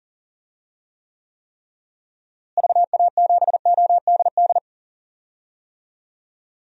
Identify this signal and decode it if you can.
Morse code, heard as a keyed tone, about 37 wpm, 700 Hz. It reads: VA7ODD
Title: DXCC-100 Call Signs - 6 Characters - Repeated 3x - 40wpm